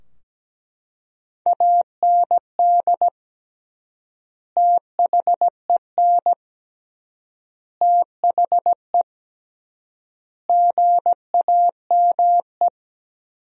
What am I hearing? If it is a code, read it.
Morse code, heard as a keyed tone, about 17 wpm, 700 Hz. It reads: AND THEN THE GAME
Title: 3x Word-Spacing - Sentences from Top 500 Words - 17wpm